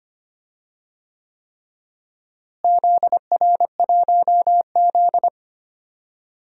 Morse code, heard as a keyed tone, about 25 wpm, 700 Hz. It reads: ZR1Z
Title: DXCC-100 Call Signs - 4 Characters - Repeated 3x - 25wpm